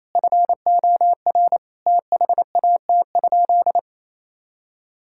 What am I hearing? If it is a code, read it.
Morse code, heard as a keyed tone, about 28 wpm, 700 Hz. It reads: FOR THAT?
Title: Sentences from Top 1000 Words - Review Part 6 - 28wpm